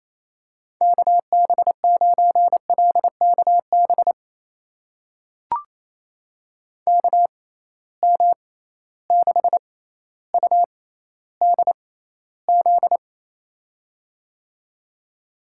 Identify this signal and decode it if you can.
Morse code, heard as a keyed tone, about 28 wpm, spaced as for 14 wpm, 700 Hz. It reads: KB9LKB KM6UDZ
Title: USA Call Signs - 6x Character Spacing - 28wpm